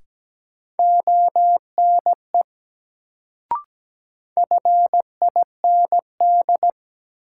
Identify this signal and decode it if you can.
Morse code, heard as a keyed tone, about 17 wpm, 700 Hz. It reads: ONE FIND
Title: Sentences from Top 100 Words - New Word Drill - 17wpm